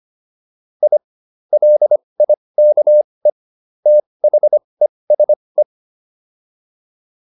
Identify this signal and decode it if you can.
Morse code, heard as a keyed tone, about 25 wpm, spaced as for 20 wpm, 600 Hz. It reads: I LIKE THESE